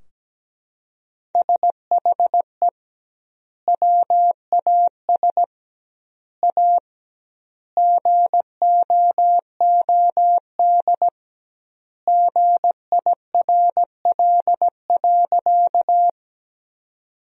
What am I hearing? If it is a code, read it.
Morse code, heard as a keyed tone, about 17 wpm, 700 Hz. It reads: SHE WAS A GOOD GIRL.
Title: 2x Word-Spacing - Sentences from Top 400 Words - 17wpm